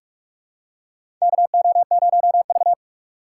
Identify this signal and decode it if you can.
Morse code, heard as a keyed tone, about 41 wpm, 700 Hz. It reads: KO0V
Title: USA Call Signs - 4 Characters - 4x Character Spacing - 45wpm